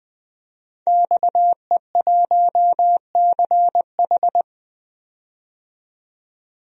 Morse code, heard as a keyed tone, about 20 wpm, 700 Hz. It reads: XE1CH
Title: DXCC-100 Call Signs - 5 Characters - 20wpm